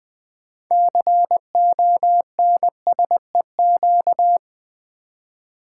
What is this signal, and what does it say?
Morse code, heard as a keyed tone, about 20 wpm, 700 Hz. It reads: CONSEQ
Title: Build A Word - 6 Letters At A Time - 12-Letter Words - 20wpm